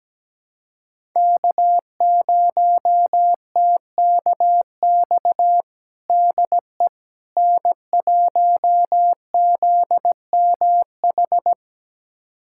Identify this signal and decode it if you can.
Morse code, heard as a keyed tone, about 17 wpm, 700 Hz. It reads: K0TKX DE N1ZMH
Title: USA Callsign DE Callsign - 5 Characters - 17wpm